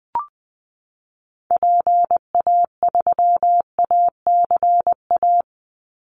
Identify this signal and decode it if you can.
Morse code, heard as a keyed tone, about 20 wpm, 700 Hz. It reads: PA3ACA
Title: Call Signs Repeated 3x - 20wpm